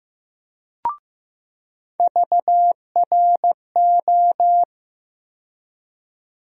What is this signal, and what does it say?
Morse code, heard as a keyed tone, about 15 wpm, 700 Hz. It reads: VRO
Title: USA Call Sign Postfixes - 3 Letters 15wpm